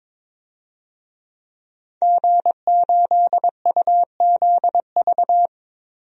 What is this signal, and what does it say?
Morse code, heard as a keyed tone, about 22 wpm, 700 Hz. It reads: G8UZV
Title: DXCC-100 Call Signs - 5 Characters - 22wpm